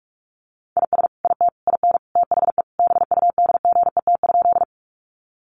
Morse code, heard as a keyed tone, about 44 wpm, 700 Hz. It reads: IS IT IN THE BUDGET?
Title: Sentences from Top 1200 Words 45wpm